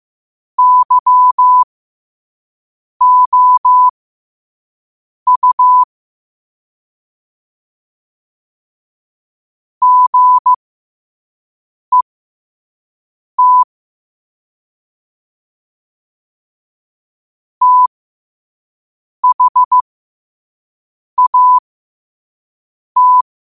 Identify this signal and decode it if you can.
Morse code, heard as a keyed tone, about 15 wpm, spaced as for 5 wpm, 1000 Hz. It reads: YOU GET THAT
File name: qotdfiles / 5wpm